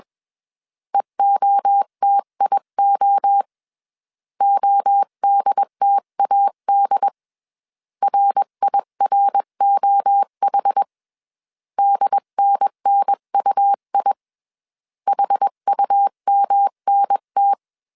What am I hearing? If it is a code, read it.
Morse code, heard as a keyed tone, about 21 wpm, 800 Hz. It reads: EOTIO ODTAD RIROH DNNUI HUMNT
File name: CW / DET / 1 3 1500C S B 20wpm 1754